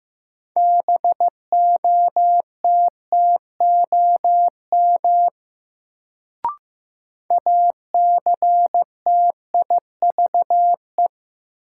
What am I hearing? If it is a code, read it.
Morse code, heard as a keyed tone, about 16 wpm, 700 Hz. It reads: BOTTOM ACTIVE
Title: Sentences from Top 900 Words - New Word Drill - 15wpm